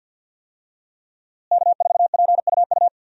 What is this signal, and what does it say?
Morse code, heard as a keyed tone, about 49 wpm, 700 Hz. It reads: K4PUU
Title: USA Call Signs - 5 Characters - Repeated 3x - 50wpm